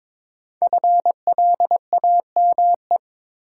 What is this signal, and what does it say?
Morse code, heard as a keyed tone, about 22 wpm, 700 Hz. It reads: FLAME
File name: 5 Letter Words - First 3 Letters Given - 22wpm